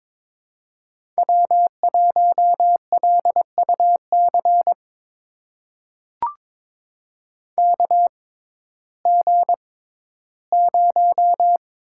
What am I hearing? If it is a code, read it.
Morse code, heard as a keyed tone, about 21 wpm, spaced as for 13 wpm, 700 Hz. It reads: W1LUC KG0
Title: USA Call Signs - 5 Characters - 6x Character Spacing - 22wpm